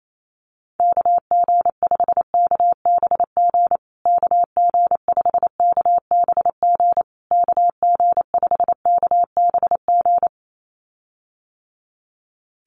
Morse code, heard as a keyed tone, about 28 wpm, 700 Hz. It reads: KG5KBG KG5KBG KG5KBG
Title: USA Call Signs Repeated 3x 28wpm